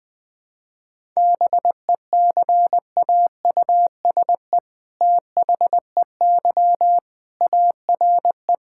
Morse code, heard as a keyed tone, about 20 wpm, 700 Hz. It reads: BECAUSE THEY ARE
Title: All-Age Appropriate Jokes - 20wpm